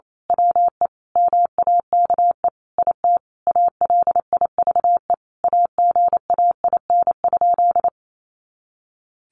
Morse code, heard as a keyed tone, about 28 wpm, 700 Hz. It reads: WE MAKE IT ALIVE AGAIN?
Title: A Sound Of Thunder 28wpm